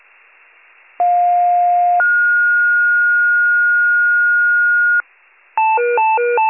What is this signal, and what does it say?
Two-tone page, heard as a one-second tone, then a three-second tone, about 710 then 1430 Hz. It is heard from 1.0 to 5.0 s.